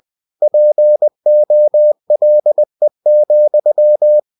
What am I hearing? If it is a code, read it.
Morse code, heard as a keyed tone, about 20 wpm, 600 Hz. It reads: POLE,
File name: cw-books / winnie-the-pooh / winnie-20wpm-page-0080